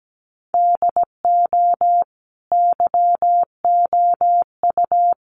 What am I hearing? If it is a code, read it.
Morse code, heard as a keyed tone, about 17 wpm, 700 Hz. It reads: DO YOU
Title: Sentences from Top 1400 Words 17wpm